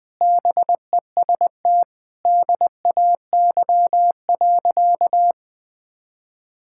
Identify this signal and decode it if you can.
Morse code, heard as a keyed tone, about 20 wpm, 700 Hz. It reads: BEST DAY.
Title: Sentences with 4-Word Phrases - Spoken Beginning - 20wpm